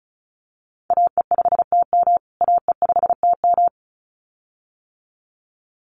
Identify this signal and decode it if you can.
Morse code, heard as a keyed tone, about 34 wpm, 700 Hz. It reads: AE5TM AE5TM
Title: USA Call Signs Repeated 2x 35wpm